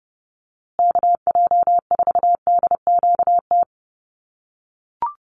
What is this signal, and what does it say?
Morse code, heard as a keyed tone, about 30 wpm, 700 Hz. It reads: KJ4DQT
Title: USA Call Signs Repeated 3x 30wpm